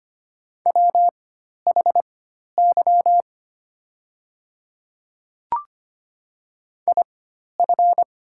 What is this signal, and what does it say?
Morse code, heard as a keyed tone, about 25 wpm, spaced as for 12 wpm, 700 Hz. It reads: WHY IF
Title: 4x Character Spacing - Top 200 Words - 25wpm